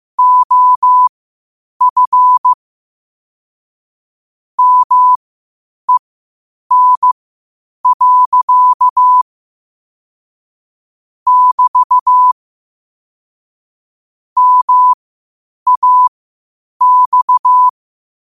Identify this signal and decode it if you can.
Morse code, heard as a keyed tone, about 15 wpm, spaced as for 8 wpm, 1000 Hz. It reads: OF MEN. <BT> MAX
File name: qotdfiles / 8wpm